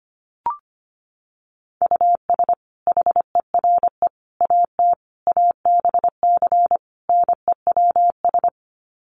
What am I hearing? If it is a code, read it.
Morse code, heard as a keyed tone, about 25 wpm, 700 Hz. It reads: US HERE AT ABC NEWS
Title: Sets of 5 Words 25wpm